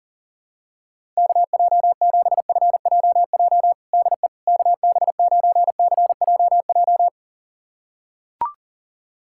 Morse code, heard as a keyed tone, about 37 wpm, 700 Hz. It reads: KJ7FJJ DE KB9CJJ
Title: USA Callsign DE Callsign - 6 Characters - 40wpm